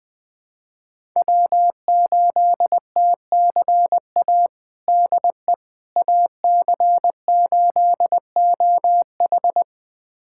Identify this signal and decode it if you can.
Morse code, heard as a keyed tone, about 20 wpm, 700 Hz. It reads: W8TCA DE AC8OH
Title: USA Callsign DE Callsign - 5 Characters - 20wpm